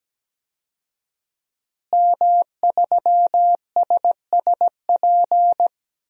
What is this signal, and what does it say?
Morse code, heard as a keyed tone, about 17 wpm, 700 Hz. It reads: M3SSP
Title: DXCC-100 Call Signs - 5 Characters - Repeated 3x - 17wpm